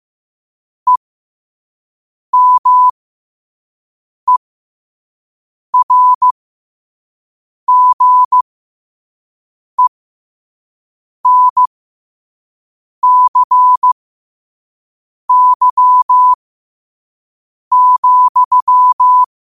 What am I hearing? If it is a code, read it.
Morse code, heard as a keyed tone, about 15 wpm, spaced as for 5 wpm, 1000 Hz. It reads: EMERGENCY,